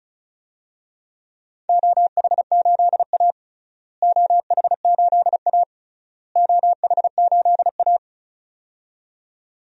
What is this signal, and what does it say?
Morse code, heard as a keyed tone, about 34 wpm, 700 Hz. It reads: OH8A OH8A OH8A
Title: DXCC-100 Call Signs - 4 Characters - Repeated 3x - 35wpm